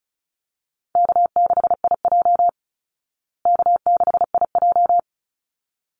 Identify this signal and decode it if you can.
Morse code, heard as a keyed tone, about 34 wpm, 700 Hz. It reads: K6IJ K6IJ
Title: Easy Call Signs Repeated 2x - 35wpm